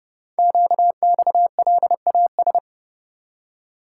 Morse code, heard as a keyed tone, about 28 wpm, 700 Hz. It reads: QXLAS
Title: Code Groups - 5 Letters - 30wpm